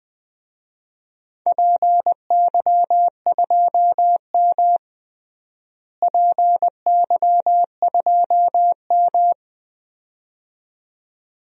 Morse code, heard as a keyed tone, about 20 wpm, 700 Hz. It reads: PY2M PY2M
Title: DXCC-100 Call Signs - 4 Characters - Repeated 2x - 20wpm